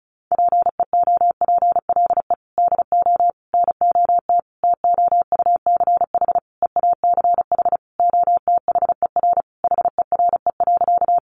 Morse code, heard as a keyed tone, about 34 wpm, 700 Hz. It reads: PEOPLE DO NOT TOUCH EACH OTHER HERE.
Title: Sentences from Top 1200 Words 35wpm